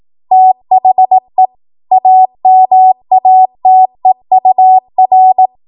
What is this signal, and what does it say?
Morse code, heard as a keyed tone, about 18 wpm, 750 Hz. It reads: THE AMATEUR
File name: Morse / 18WPM